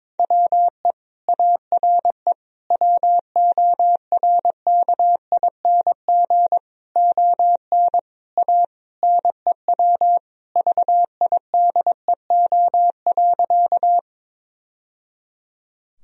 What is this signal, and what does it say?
Morse code, heard as a keyed tone, about 22 wpm, 700 Hz. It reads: WE ARE WORKING ON A NEW VIDEO.
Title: Sentences from Top 1000 Words - Review Part 4 - 22wpm